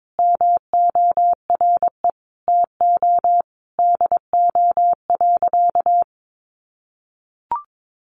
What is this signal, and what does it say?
Morse code, heard as a keyed tone, about 22 wpm, 700 Hz. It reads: MORE TO DO.
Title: Sentences from Top 100 Words 22wpm